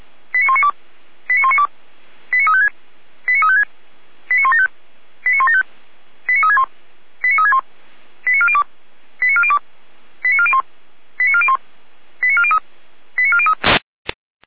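Five-tone selective call (ZVEI), heard as quick runs of five tones, tones of about 80 ms. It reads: 89192 89192 89356 89356 89175 89175 89361 89361 89402 89402 89401 89401 89403 89403
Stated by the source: ZVEI five-tone call, code 89409, received by SDR.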